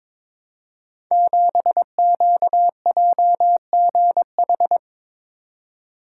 Morse code, heard as a keyed tone, about 22 wpm, 700 Hz. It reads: 7QJGH
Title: Code Groups - 5 Letter:Numbers - 22wpm